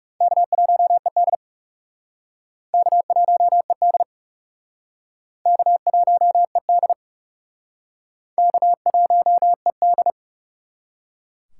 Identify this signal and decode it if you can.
Morse code, heard as a keyed tone, about 37 wpm, 700 Hz. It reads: K1ED K1ED K1ED K1ED
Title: Speed Racing - USA Call Signs 30wpm